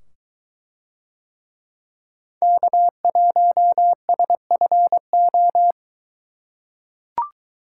Morse code, heard as a keyed tone, about 23 wpm, 700 Hz. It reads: K1SFO